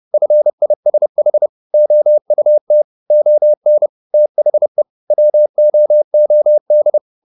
Morse code, heard as a keyed tone, about 30 wpm, 600 Hz. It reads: FISH OUT ON THE WOOD